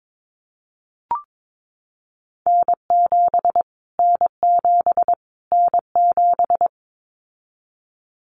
Morse code, heard as a keyed tone, about 22 wpm, 700 Hz. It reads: N7 N7 N7
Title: Callsign Prefixes for Top 25 DXCC - Repeated 3x - 22wpm